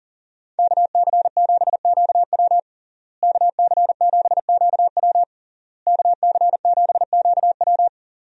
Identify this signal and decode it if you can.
Morse code, heard as a keyed tone, about 40 wpm, 700 Hz. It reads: KC7QW KC7QW KC7QW
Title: USA Call Signs - 5 Characters - Repeated 3x - 40wpm